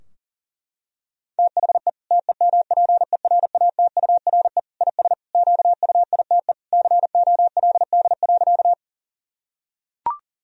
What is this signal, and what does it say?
Morse code, heard as a keyed tone, about 40 wpm, 700 Hz. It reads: THE TEMPERATURE IS QUITE COLD.
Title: Sentences from Top 1000 Words - Review Part 4 - 40wpm